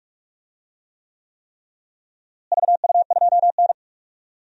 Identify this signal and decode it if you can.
Morse code, heard as a keyed tone, about 46 wpm, 700 Hz. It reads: VU2N
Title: DXCC-100 Call Signs - 4 Characters - Repeated 3x - 45wpm